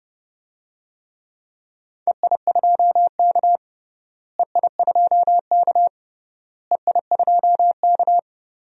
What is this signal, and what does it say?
Morse code, heard as a keyed tone, about 30 wpm, 700 Hz. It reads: EI2K EI2K EI2K
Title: DXCC-100 Call Signs - 4 Characters - Repeated 3x - 30wpm